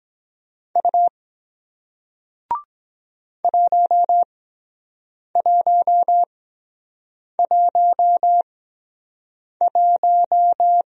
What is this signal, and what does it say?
Morse code, heard as a keyed tone, about 23 wpm, spaced as for 7 wpm, 700 Hz. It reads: U 1111